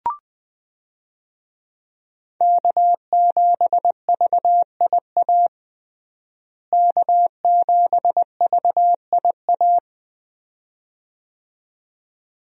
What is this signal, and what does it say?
Morse code, heard as a keyed tone, about 20 wpm, 700 Hz. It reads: K7VIA K7VIA
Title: USA Call Signs - 5 Characters - Repeated 2x - 20wpm